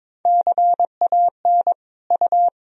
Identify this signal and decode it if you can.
Morse code, heard as a keyed tone, about 22 wpm, 700 Hz. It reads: CAN U
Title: Sentences from Top 400 Words and Abbreviations 22wpm